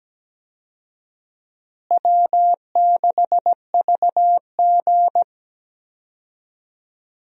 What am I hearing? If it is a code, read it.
Morse code, heard as a keyed tone, about 17 wpm, 700 Hz. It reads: W6VG
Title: USA Call Signs - 4 Characters - Repeated 2x - 17wpm